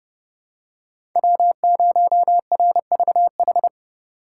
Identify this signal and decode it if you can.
Morse code, heard as a keyed tone, about 30 wpm, 700 Hz. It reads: W0RVH
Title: USA Call Signs - 2x Character Spacing - 30wpm